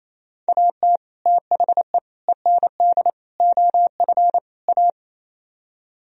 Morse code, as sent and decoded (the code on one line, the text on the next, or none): .- - / - .... / . -. -.. / --- ..-. / .-
AT THE END OF A